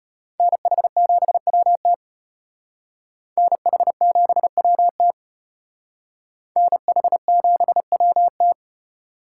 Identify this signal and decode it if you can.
Morse code, heard as a keyed tone, about 32 wpm, 700 Hz. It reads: NH7WT NH7WT NH7WT